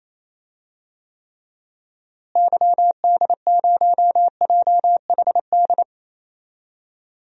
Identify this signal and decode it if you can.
Morse code, heard as a keyed tone, about 28 wpm, 700 Hz. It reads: YD0JHD
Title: DXCC-100 Call Signs - 6 Characters - Repeated 2x - 28wpm